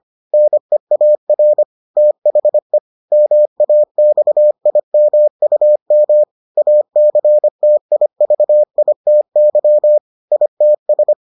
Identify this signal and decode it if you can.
Morse code, heard as a keyed tone, about 25 wpm, 600 Hz. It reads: NEAR THE MAXIMUM ACTIVITY ITS